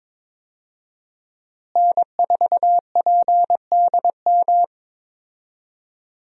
Morse code, heard as a keyed tone, about 22 wpm, 700 Hz. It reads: N4PDM
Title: USA Call Signs - 5 Characters - Repeated 2x - 22wpm